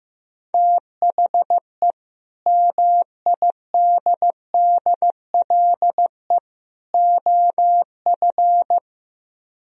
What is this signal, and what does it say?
Morse code, heard as a keyed tone, about 15 wpm, 700 Hz. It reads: THE MIDDLE OF
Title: Speed-Racing - Sets of 3 Words 15wpm